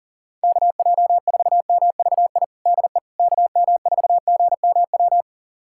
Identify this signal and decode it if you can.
Morse code, heard as a keyed tone, about 37 wpm, 700 Hz. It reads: KJ4MVI DE KM4GMW
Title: USA Callsign DE Callsign - 6 Characters - 40wpm